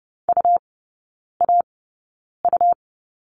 Morse code, heard as a keyed tone, about 30 wpm, spaced as for 9 wpm, 700 Hz. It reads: UAU
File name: Lesson 17e - Letter u - Three Characters 30wpm